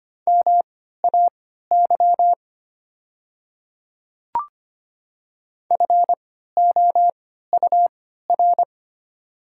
Morse code, heard as a keyed tone, about 25 wpm, spaced as for 14 wpm, 700 Hz. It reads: MAY FOUR
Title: 3x Character Spacing - Top 300 Words - 25wpm